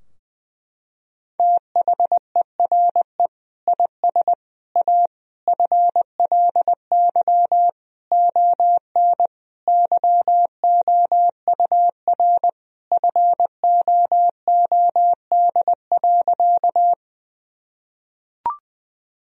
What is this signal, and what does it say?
Morse code, heard as a keyed tone, about 20 wpm, 700 Hz. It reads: THERE IS A FLY ON YOUR FOOD.
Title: Sentences from Top 1000 Words - Review Part 2 - 20wpm